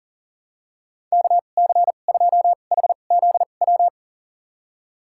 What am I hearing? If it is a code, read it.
Morse code, heard as a keyed tone, about 40 wpm, spaced as for 29 wpm, 700 Hz. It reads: KC2HZW
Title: USA Call Signs - 6 Characters - 2x Character Spacing - 40wpm